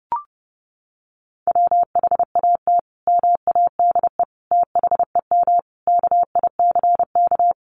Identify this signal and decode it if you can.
Morse code, heard as a keyed tone, about 30 wpm, 700 Hz. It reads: WHAT MADE THEM KICK
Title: Sentences from Top 1300 Words 30wpm